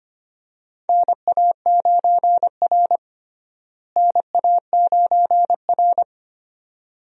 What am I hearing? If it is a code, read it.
Morse code, heard as a keyed tone, about 25 wpm, 700 Hz. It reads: NA9R NA9R
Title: USA Call Signs - 4 Characters - Repeated 2x - 25wpm